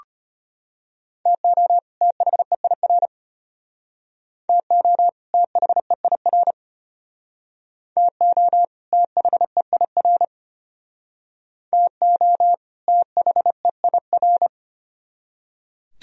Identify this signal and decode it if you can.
Morse code, heard as a keyed tone, about 32 wpm, 700 Hz. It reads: TO THEIR TO THEIR TO THEIR TO THEIR